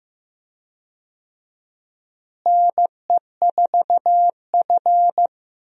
Morse code, heard as a keyed tone, about 15 wpm, 700 Hz. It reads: NE4F
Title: USA Call Signs - 4 Characters - Repeated 2x - 15wpm